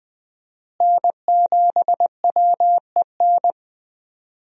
Morse code, heard as a keyed tone, about 20 wpm, 700 Hz. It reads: N7WEN
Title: USA Call Signs - 5 Characters - Repeated 3x - 20wpm